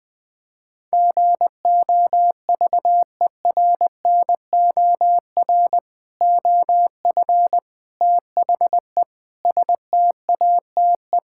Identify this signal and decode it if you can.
Morse code, heard as a keyed tone, about 20 wpm, 700 Hz. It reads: GOVERNOR OF THE STATE
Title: Sentences with 3-Word Phrases - Spoken Beginning - 20wpm